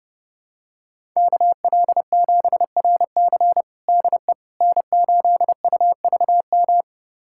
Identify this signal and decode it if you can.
Morse code, heard as a keyed tone, about 32 wpm, 700 Hz. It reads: KL7RC DE N8UVM
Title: USA Callsign DE Callsign - 5 Characters - 30wpm